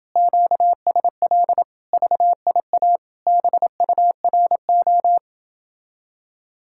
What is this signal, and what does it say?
Morse code, heard as a keyed tone, about 27 wpm, 700 Hz. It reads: QSL VIA BURO